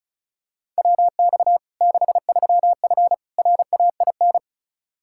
Morse code, heard as a keyed tone, about 36 wpm, 700 Hz. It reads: WX 63F RAIN
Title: Practice QSO - Exchange 2 - Info - 35wpm